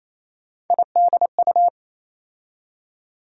Morse code, heard as a keyed tone, about 28 wpm, 700 Hz. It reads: IDU
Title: USA Call Sign Postfixes - 3 Letters 28wpm